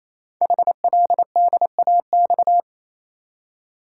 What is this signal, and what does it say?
Morse code, heard as a keyed tone, about 27 wpm, 700 Hz. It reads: HLDAX
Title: Code Groups - 5 Letters - 28wpm